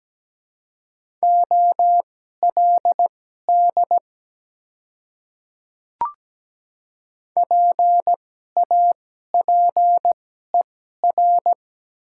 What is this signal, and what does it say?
Morse code, heard as a keyed tone, about 17 wpm, spaced as for 12 wpm, 690 Hz. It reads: OLD PAPER
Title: 2x Character Spacing - Top 300 Words - 17wpm